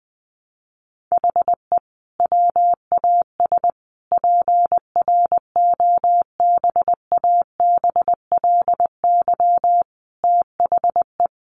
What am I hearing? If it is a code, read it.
Morse code, heard as a keyed tone, about 20 wpm, 700 Hz. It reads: HE WAS PROBABLY THE
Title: Sentences from Top 1300 Words 20wpm